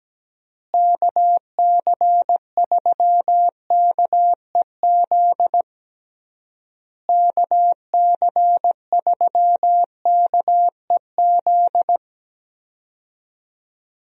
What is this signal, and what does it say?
Morse code, heard as a keyed tone, about 17 wpm, 700 Hz. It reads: KC3KEZ KC3KEZ
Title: USA Call Signs - 6 Characters - Repeated 3x - 17wpm